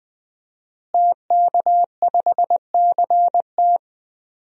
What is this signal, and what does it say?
Morse code, heard as a keyed tone, about 20 wpm, 700 Hz. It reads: TK5CT
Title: DXCC-100 Call Signs - 5 Characters - Repeated 2x - 20wpm